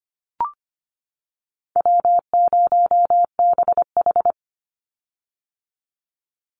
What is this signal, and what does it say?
Morse code, heard as a keyed tone, about 25 wpm, 700 Hz. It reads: W0BH
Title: Call Signs - 25wpm